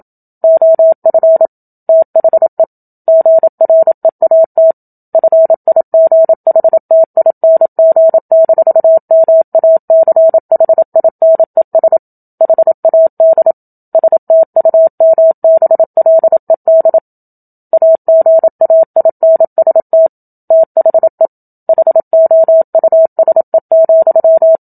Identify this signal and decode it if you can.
Morse code, heard as a keyed tone, about 27 wpm, 650 Hz. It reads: OF THE GREAT FIGHTING-MACHINES HAD STUMBLED AGAINST THE HOUSE,
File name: Book Two / 01-Chapter 1 - 27 WPM